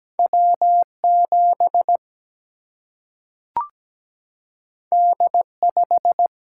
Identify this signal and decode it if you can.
Morse code, heard as a keyed tone, about 17 wpm, 700 Hz. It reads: W 7 D 5
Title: Single-Letter Single-Number 17wpm